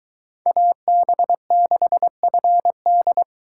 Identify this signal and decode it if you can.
Morse code, heard as a keyed tone, about 23 wpm, 700 Hz. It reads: AB6FD